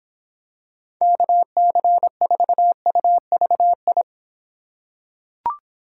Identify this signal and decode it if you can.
Morse code, heard as a keyed tone, about 26 wpm, 700 Hz. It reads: KC4UVI